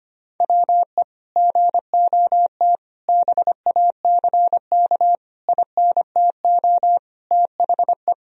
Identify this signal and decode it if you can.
Morse code, heard as a keyed tone, about 25 wpm, 700 Hz. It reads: WE GOT BACK INTO THE
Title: Sentences from Top 1000 Words - Review Part 3 - 25wpm